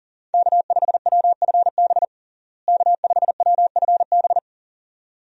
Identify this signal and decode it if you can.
Morse code, heard as a keyed tone, about 37 wpm, 700 Hz. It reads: K5WFB K5WFB
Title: USA Call Signs - 5 Characters - Repeated 2x - 40wpm